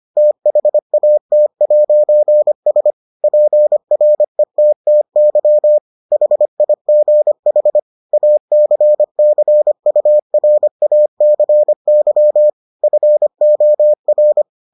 Morse code, heard as a keyed tone, about 25 wpm, 600 Hz. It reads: THAT'S PRETTY HIGH ACCURACY FOR